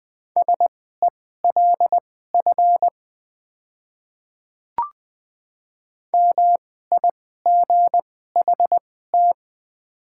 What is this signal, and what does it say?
Morse code, heard as a keyed tone, about 21 wpm, spaced as for 15 wpm, 700 Hz. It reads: SELF MIGHT
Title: 2x Character Spacing - Top 500 Words - 20wpm